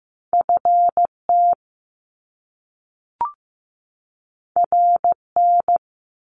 Morse code, heard as a keyed tone, about 15 wpm, 700 Hz. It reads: FT RN
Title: Top Two Letter Combinations in Words 15wpm